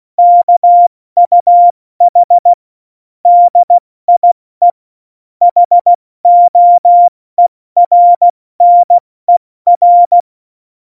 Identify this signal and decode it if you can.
Morse code, heard as a keyed tone, about 16 wpm, 700 Hz. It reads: KUH DIE HOERNER
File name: Morsekurs / Witze / Witz 14 Juli 2020, 70 F80